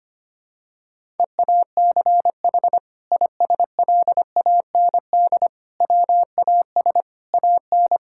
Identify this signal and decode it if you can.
Morse code, heard as a keyed tone, about 25 wpm, 700 Hz. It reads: EACH ISLAND WAS AN
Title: Sentences from Top 1000 Words - Review Part 6 - 25wpm